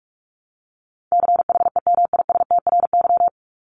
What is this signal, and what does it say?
Morse code, heard as a keyed tone, about 44 wpm, 700 Hz. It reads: CHEMISTRY
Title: Top 751-1000 Words 45wpm